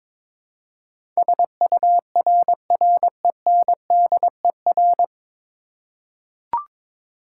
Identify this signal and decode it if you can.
Morse code, heard as a keyed tone, about 22 wpm, 700 Hz. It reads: SURRENDER
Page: Morse Code Ninja practice 22wpm